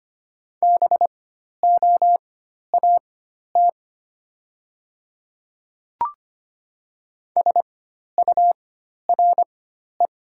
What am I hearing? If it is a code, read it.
Morse code, heard as a keyed tone, about 25 wpm, spaced as for 12 wpm, 700 Hz. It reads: BOAT SURE
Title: 4x Character Spacing - Top 500 Words - 25wpm